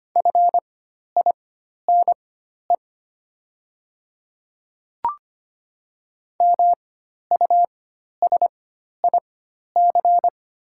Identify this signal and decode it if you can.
Morse code, heard as a keyed tone, about 25 wpm, spaced as for 12 wpm, 690 Hz. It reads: FINE MUSIC
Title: 4x Character Spacing - Top 500 Words - 25wpm